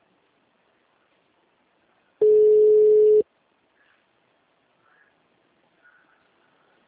Ringback tone (Europe, 425 Hz).
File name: out- -2010- - - .98384